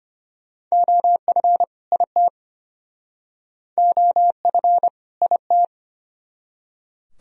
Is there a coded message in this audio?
Yes: Morse code from 0.7 to 5.7 s, heard as a keyed tone, about 33 wpm, 700 Hz. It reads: OF IT OF IT